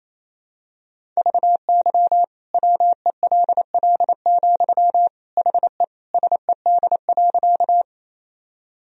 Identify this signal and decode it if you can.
Morse code, heard as a keyed tone, about 28 wpm, 700 Hz. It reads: VY WELL, HE SED.
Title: Sentences from Top 200 Words and Abbreviations 28wpm